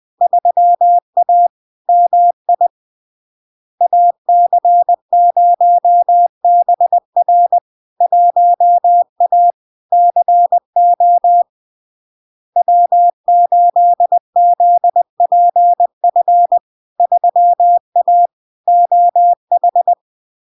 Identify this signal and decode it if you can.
Morse code, heard as a keyed tone, about 20 wpm, 700 Hz. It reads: 3A MI AC0BR 1A CO W8ZPF 3A OH
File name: LARK-FD-2022-Log-in-CW-20wpm-700-Hz